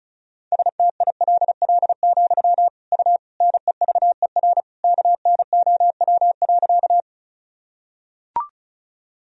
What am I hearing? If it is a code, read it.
Morse code, heard as a keyed tone, about 34 wpm, 700 Hz. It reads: STILL, U NEVER KNOW.
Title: Sentences from Top 300 Words and Abbreviations 35wpm